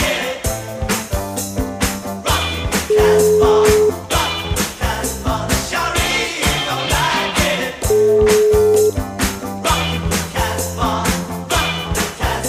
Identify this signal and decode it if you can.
Ringback tone (Europe, 425 Hz), with noise.